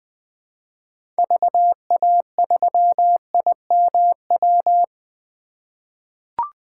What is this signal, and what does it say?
Morse code, heard as a keyed tone, about 20 wpm, 700 Hz. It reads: VA3IMW
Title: 2023 POTA Hunter Call Signs - 20wpm